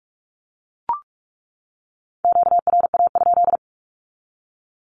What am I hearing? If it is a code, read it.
Morse code, heard as a keyed tone, about 44 wpm, 700 Hz. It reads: QRA?
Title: Q-Codes 45wpm